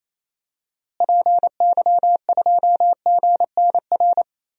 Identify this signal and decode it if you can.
Morse code, heard as a keyed tone, about 27 wpm, 700 Hz. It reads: PY2GNR
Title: DXCC-100 Call Signs - 6 Characters - Repeated 2x - 28wpm